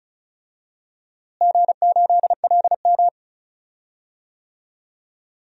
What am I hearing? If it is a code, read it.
Morse code, heard as a keyed tone, about 36 wpm, 700 Hz. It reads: G8LM
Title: DXCC-100 Call Signs - 4 Characters - Repeated 3x - 35wpm